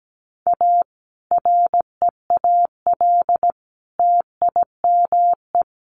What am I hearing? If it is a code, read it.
Morse code, heard as a keyed tone, about 17 wpm, 700 Hz. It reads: A REAL TIME
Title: A Sound Of Thunder 17wpm